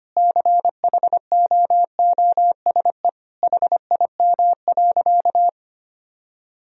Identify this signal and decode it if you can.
Morse code, heard as a keyed tone, about 25 wpm, 700 Hz. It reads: CHOOSE HIM.
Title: Sentences from Top 1000 Words - Review Part 4 - 25wpm